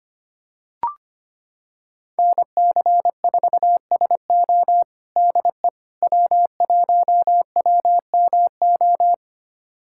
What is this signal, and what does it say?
Morse code, heard as a keyed tone, about 25 wpm, 700 Hz. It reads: NC4SO DE W1WMO
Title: USA Callsign DE Callsign - 5 Characters - 25wpm